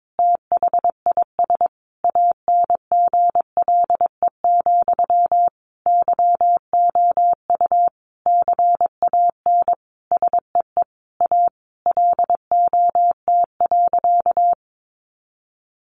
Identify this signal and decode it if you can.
Morse code, heard as a keyed tone, about 22 wpm, 700 Hz. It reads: THIS ANGLE, YOU CAN SEE A LOT.
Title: Sentences from Top 1500 Words 22wpm